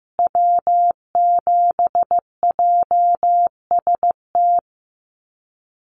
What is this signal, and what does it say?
Morse code, heard as a keyed tone, about 14 wpm, 700 Hz. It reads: W7JST
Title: USA Call Signs Repeated 2x 15wpm